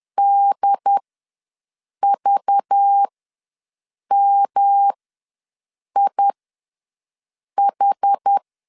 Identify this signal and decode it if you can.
Morse code, heard as a keyed tone, about 10 wpm, spaced as for 6 wpm, 800 Hz. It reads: DVMIH
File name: LEZIONI / 5 LBVC / 5 LBVC ws 10wpm 3733